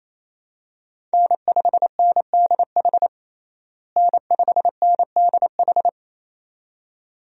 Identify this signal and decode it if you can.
Morse code, heard as a keyed tone, about 28 wpm, 700 Hz. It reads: N5NDH N5NDH
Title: USA Call Signs - 5 Characters - Repeated 2x - 28wpm